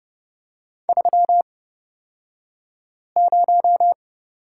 Morse code, heard as a keyed tone, about 30 wpm, spaced as for 5 wpm, 700 Hz. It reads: 30
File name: Lesson 36d - BT - Two Characters 30wpm